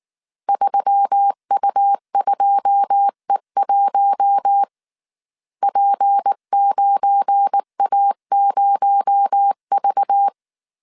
Morse code, heard as a keyed tone, about 19 wpm, 800 Hz. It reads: 3U2E1 P9A0V